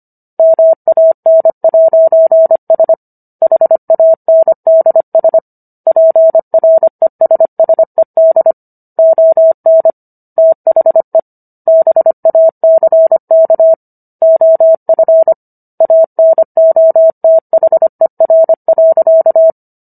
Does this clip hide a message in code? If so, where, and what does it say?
Yes: Morse code from 0.4 to 19.5 s, heard as a keyed tone, about 25 wpm, 650 Hz. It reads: MAN'S HANDS PRESSED ON THE BACK OF ANOTHER.